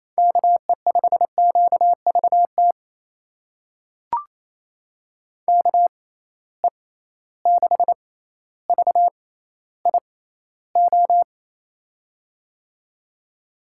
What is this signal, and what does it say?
Morse code, heard as a keyed tone, about 28 wpm, spaced as for 14 wpm, 700 Hz. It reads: KE5QVT KE6VIO
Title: USA Call Signs - 6 Characters - 6x Character Spacing - 28wpm